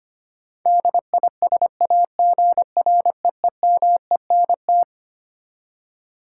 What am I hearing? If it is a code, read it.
Morse code, heard as a keyed tone, about 25 wpm, 700 Hz. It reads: DISAGREEMENT
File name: Twelve Letter Words - 25wpm